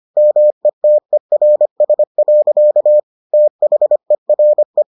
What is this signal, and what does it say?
Morse code, heard as a keyed tone, about 25 wpm, 600 Hz. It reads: METERS. THERE